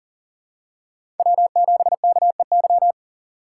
Morse code, heard as a keyed tone, about 40 wpm, 700 Hz. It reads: W7KEY
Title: USA Call Signs - 2x Character Spacing - 40wpm